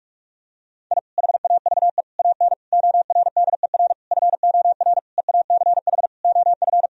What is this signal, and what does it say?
Morse code, heard as a keyed tone, about 42 wpm, 700 Hz. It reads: I HAVE AN ORDER FOR EACH OF
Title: Sentences from Top 1000 Words - Review Part 2 - 45wpm